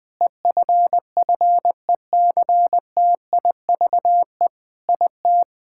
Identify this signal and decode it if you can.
Morse code, heard as a keyed tone, about 20 wpm, 700 Hz. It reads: EFFECTIVE IT
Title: Sentences from Top 1000 Words - Review Part 6 - 20wpm